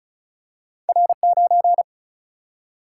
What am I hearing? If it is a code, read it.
Morse code, heard as a keyed tone, about 36 wpm, 700 Hz. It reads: R 9
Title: Single-Letter Single-Number 35wpm